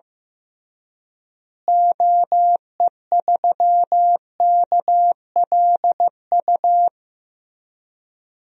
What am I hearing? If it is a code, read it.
Morse code, heard as a keyed tone, about 15 wpm, 700 Hz. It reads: OE3KLU
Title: DXCC-100 Call Signs - 6 Characters - Repeated 3x - 15wpm